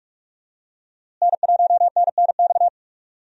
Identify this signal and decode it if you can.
Morse code, heard as a keyed tone, about 46 wpm, 700 Hz. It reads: N1NNX
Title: USA Call Signs - 5 Characters - Repeated 3x - 45wpm